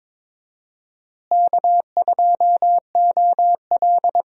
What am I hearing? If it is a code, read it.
Morse code, heard as a keyed tone, about 21 wpm, 700 Hz. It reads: K2OL
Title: USA Call Signs - 4 Characters - Repeated 3x - 22wpm